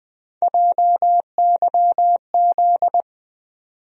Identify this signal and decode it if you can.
Morse code, heard as a keyed tone, about 20 wpm, 700 Hz. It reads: JYZ
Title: USA Call Sign Postfixes - 3 Letters 20wpm